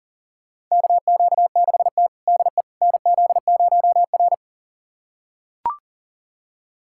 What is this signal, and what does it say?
Morse code, heard as a keyed tone, about 37 wpm, 700 Hz. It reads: KQ6T DE NZ0R
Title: USA Callsign DE Callsign - 4 Characters - 40wpm